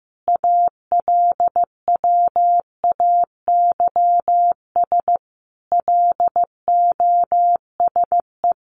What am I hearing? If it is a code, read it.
Morse code, heard as a keyed tone, about 15 wpm, 700 Hz. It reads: ALWAYS LOSE
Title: Sentences from Top 1300 Words 15wpm